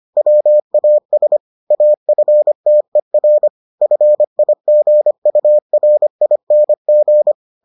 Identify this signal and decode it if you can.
Morse code, heard as a keyed tone, about 25 wpm, 600 Hz. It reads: WAS AFTER FIGURING